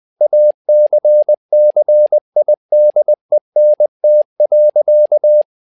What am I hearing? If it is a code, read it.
Morse code, heard as a keyed tone, about 20 wpm, 600 Hz. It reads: ACCIDENT.
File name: learning center / tower trouble / tower trouble 20wpm